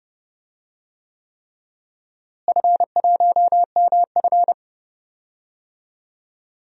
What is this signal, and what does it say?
Morse code, heard as a keyed tone, about 31 wpm, 700 Hz. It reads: F1MF
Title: DXCC-100 Call Signs - 4 Characters - Repeated 3x - 30wpm